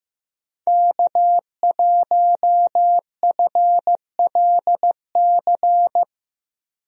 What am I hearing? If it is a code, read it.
Morse code, heard as a keyed tone, about 15 wpm, 700 Hz. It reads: K1FLC
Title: USA Call Signs - 5 Characters - Repeated 2x - 15wpm